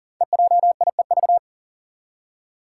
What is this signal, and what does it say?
Morse code, heard as a keyed tone, about 40 wpm, 700 Hz. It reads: EJIEV
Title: Five Challenging Letter-Number Combinations 40wpm